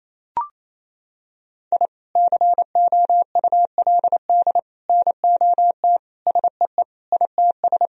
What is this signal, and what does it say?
Morse code, heard as a keyed tone, about 28 wpm, 700 Hz. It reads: I COULD NOT SEE ITS
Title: Sentences from Top 1000 Words - Review Part 3 - 28wpm